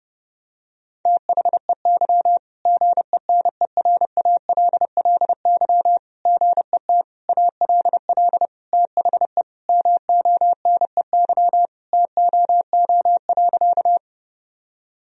Morse code, heard as a keyed tone, about 28 wpm, 700 Hz. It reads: THEY GENERALLY GET ALL THE MONEY TOO.
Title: Sentences from Top 1000 Words - Review Part 4 - 30wpm